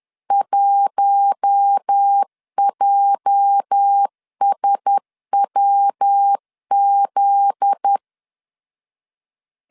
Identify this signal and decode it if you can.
Morse code, heard as a keyed tone, about 10 wpm, 800 Hz. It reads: 1JSWZ
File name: CW / DET / 1 9 1500C S A 10wpm 4511